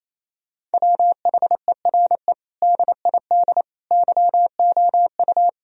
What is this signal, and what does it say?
Morse code, heard as a keyed tone, about 28 wpm, 700 Hz. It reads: WHERE DID YOU
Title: Sentences from Top 1000 Words - Review Part 3 - 28wpm